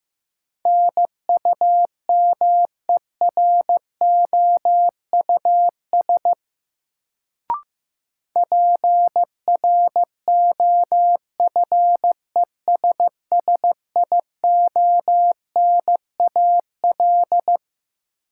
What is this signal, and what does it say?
Morse code, heard as a keyed tone, about 15 wpm, 700 Hz. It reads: NUMEROUS PROFESSIONAL
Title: Sentences from Top 1400 Words - New Word Drill Alt - 15wpm